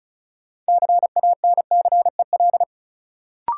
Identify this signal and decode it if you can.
Morse code, heard as a keyed tone, about 34 wpm, 700 Hz. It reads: CANCEL
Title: Sentences from Top 1200 Words - New Word Drill - 35wpm